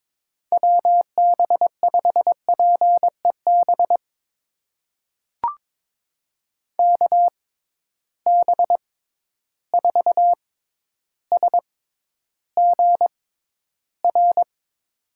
Morse code, heard as a keyed tone, about 22 wpm, spaced as for 11 wpm, 700 Hz. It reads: WB5PEB KB4SGR
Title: USA Call Signs - 6 Characters - 6x Character Spacing - 22wpm